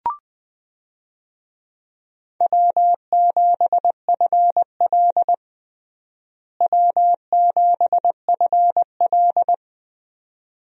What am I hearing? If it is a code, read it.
Morse code, heard as a keyed tone, about 21 wpm, 700 Hz. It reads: W7FL W7FL
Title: USA Call Signs - 4 Characters - Repeated 2x - 20wpm